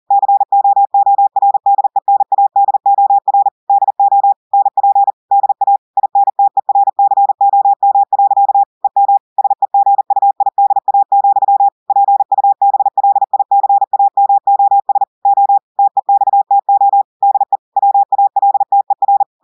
Morse code, heard as a keyed tone, about 40 wpm, 800 Hz. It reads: COORDENADOR DO NP DA INTERCOM. EM SEGUIDA, PUBLICAMOS O TEXTO DE WALTER
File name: audios / portugais / portugais texte en clair 1 vitesse 40 mots minutes